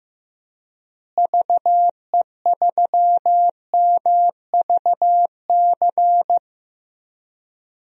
Morse code, heard as a keyed tone, about 15 wpm, 700 Hz. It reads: VE3MVC
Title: DXCC-100 Call Signs - 6 Characters - Repeated 3x - 15wpm